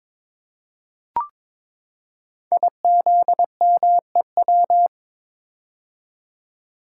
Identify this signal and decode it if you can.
Morse code, heard as a keyed tone, about 22 wpm, 700 Hz. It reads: IZMEW
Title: Code Groups - 5 Letters - 22wpm